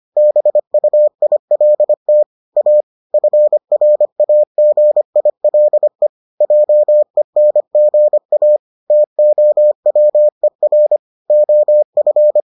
Morse code, heard as a keyed tone, about 25 wpm, 600 Hz. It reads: BUILT A FRAGILE JENGA TOWER OF